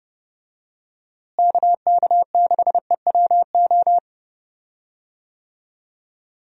Morse code, heard as a keyed tone, about 30 wpm, 700 Hz. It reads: KK6EWO
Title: USA Call Signs - 6 Characters - Repeated 2x - 30wpm